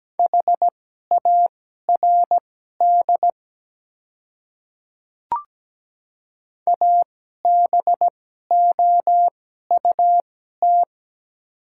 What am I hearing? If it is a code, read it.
Morse code, heard as a keyed tone, about 17 wpm, spaced as for 12 wpm, 700 Hz. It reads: HARD ABOUT
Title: 2x Character Spacing - Top 300 Words - 17wpm